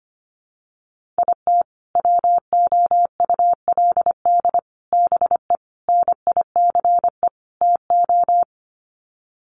Morse code, heard as a keyed tone, about 25 wpm, 700 Hz. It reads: IT WOULD BE NICE TO
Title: Sets of 5 Words Encore2 25wpm